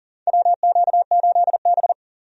Morse code, heard as a keyed tone, about 37 wpm, 700 Hz. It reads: WQ8B
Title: USA Call Signs - 4 Characters - 40wpm